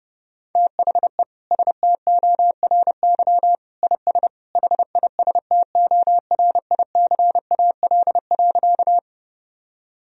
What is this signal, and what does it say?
Morse code, heard as a keyed tone, about 30 wpm, 700 Hz. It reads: THE STORY IS HISTORICAL.
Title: Sentences from Top 1000 Words - Review Part 4 - 30wpm